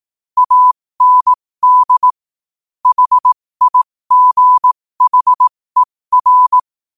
Morse code, heard as a keyed tone, about 18 wpm, spaced as for 15 wpm, 1000 Hz. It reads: AND HIGHER
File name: qotdfiles / 15wpm